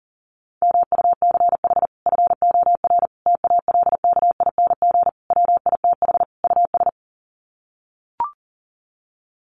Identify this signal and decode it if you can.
Morse code, heard as a keyed tone, about 40 wpm, 700 Hz. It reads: MUCH FOR TALKING WITH US
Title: Sets of 5 Words Encore 40wpm